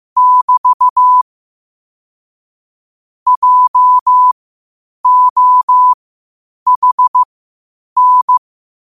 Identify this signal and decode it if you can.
Morse code, heard as a keyed tone, about 15 wpm, spaced as for 8 wpm, 1000 Hz. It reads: <BT> JOHN
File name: qotdfiles / 8wpm